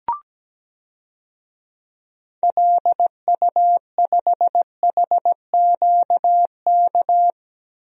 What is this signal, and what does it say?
Morse code, heard as a keyed tone, about 17 wpm, 700 Hz. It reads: LU5HQK
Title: DXCC-100 Call Signs - 6 Characters - Repeated 2x - 17wpm